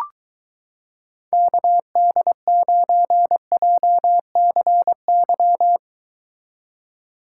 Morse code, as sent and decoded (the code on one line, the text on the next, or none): -.- -.. ----. .--- -.-. -.--
KD9JCY